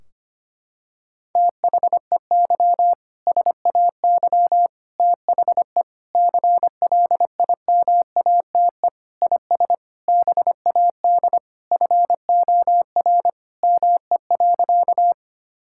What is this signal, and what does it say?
Morse code, heard as a keyed tone, about 25 wpm, 700 Hz. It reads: THEY SAY THE CLIMATE IS BAD FOR ME.
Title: Sentences from Top 1600 Words 25wpm